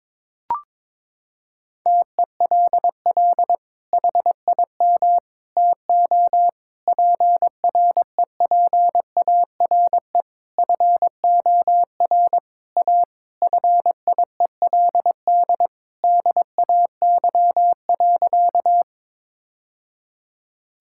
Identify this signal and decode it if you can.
Morse code, heard as a keyed tone, about 22 wpm, 700 Hz. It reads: TELL HIM TO PREPARE FOR A FIELD DAY.
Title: Sentences from Top 1000 Words - Review Part 6 - 22wpm